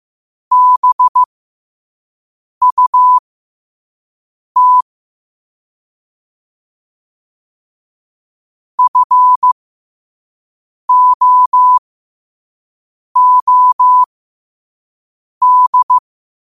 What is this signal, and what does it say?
Morse code, heard as a keyed tone, about 15 wpm, spaced as for 5 wpm, 1000 Hz. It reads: BUT FOOD
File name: qotdfiles / 5wpm